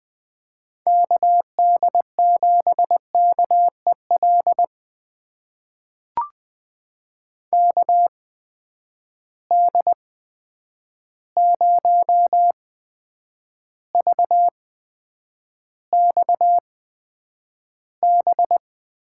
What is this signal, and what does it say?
Morse code, heard as a keyed tone, about 20 wpm, spaced as for 9 wpm, 700 Hz. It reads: KD7KEL KD0VXB
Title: USA Call Signs - 6 Characters - 8x Character Spacing - 20wpm